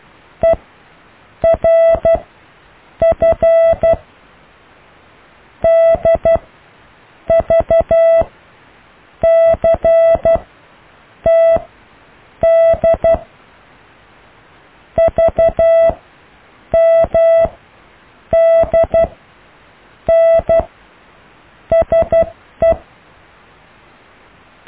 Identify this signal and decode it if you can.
Morse code, heard as a keyed tone, about 12 wpm, 660 Hz. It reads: E R F D V C T D V M D N SE
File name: cw / IJLUM+CDRO+EKTPV+BFHNW